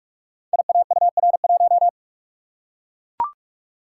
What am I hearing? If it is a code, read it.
Morse code, heard as a keyed tone, about 44 wpm, 700 Hz. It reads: IAUR1
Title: Code Groups - 5 Letter:Numbers - 45wpm